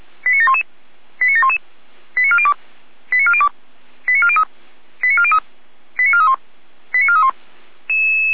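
Five-tone selective call (ZVEI), heard as quick runs of five tones, tones of about 70 ms. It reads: 89711 89711 89402 89402 89403 89403 89431 89431
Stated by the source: ZVEI five-tone selective call 89735, received on SDR.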